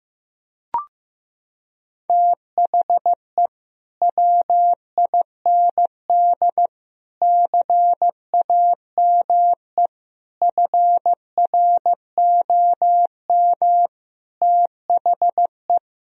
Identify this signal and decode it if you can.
Morse code, heard as a keyed tone, about 15 wpm, 700 Hz. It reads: THE WIND CAME FROM THE
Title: Sentences from Top 1000 Words - Review Part 2 - 15wpm